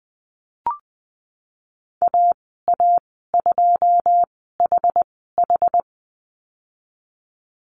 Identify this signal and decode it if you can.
Morse code, heard as a keyed tone, about 20 wpm, spaced as for 15 wpm, 700 Hz. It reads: AA2HH
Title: 082 Lesson 14g - Number 2 - Call Signs 20wpm